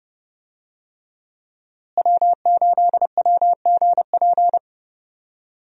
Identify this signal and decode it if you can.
Morse code, heard as a keyed tone, about 32 wpm, 700 Hz. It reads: W8WGP
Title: USA Call Signs - 5 Characters - Repeated 3x - 30wpm